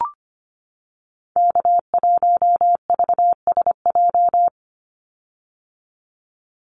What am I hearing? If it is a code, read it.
Morse code, heard as a keyed tone, about 25 wpm, 700 Hz. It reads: K1VSJ
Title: Call Signs - 25wpm